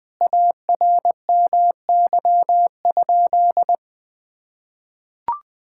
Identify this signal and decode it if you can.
Morse code, heard as a keyed tone, about 20 wpm, 700 Hz. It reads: ARMY?
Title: Sentences from Top 1000 Words - Review Part 5 - 20wpm